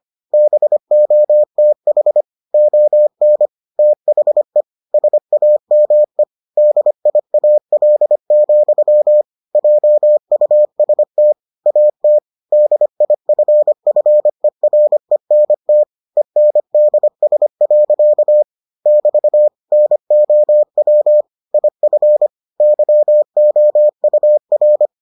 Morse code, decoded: BOTH ON THE SAME DIAL, JUST AT DIFFERENT ENDS. <BT> NOW IF YOUR

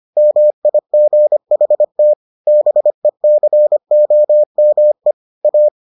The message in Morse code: MIGHT BECOME A

